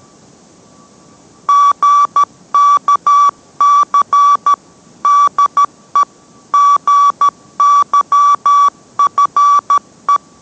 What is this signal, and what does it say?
Morse code, heard as a keyed tone, about 13 wpm, 1150 Hz. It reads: GKC DE GYFE